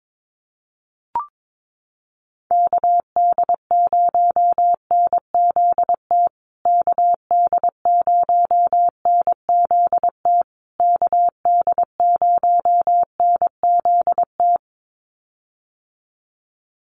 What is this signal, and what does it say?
Morse code, heard as a keyed tone, about 22 wpm, 700 Hz. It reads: KD0NZT KD0NZT KD0NZT
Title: USA Call Signs Repeated 3x 22wpm